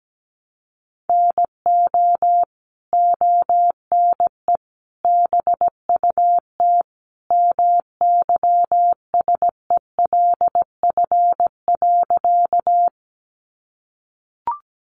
Morse code, heard as a keyed tone, about 17 wpm, 700 Hz. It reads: NO ONE BUT MYSELF.
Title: Sentences from Top 1300 Words 17wpm